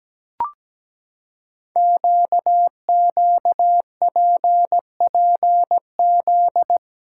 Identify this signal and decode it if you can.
Morse code, heard as a keyed tone, about 17 wpm, 700 Hz. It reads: QQPPZ
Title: Code Groups - 5 Letters - 17wpm